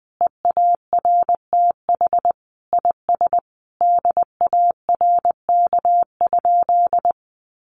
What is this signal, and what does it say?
Morse code, heard as a keyed tone, about 20 wpm, 700 Hz. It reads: EARTH IS DARK?
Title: Sentences from Top 900 Words 20wpm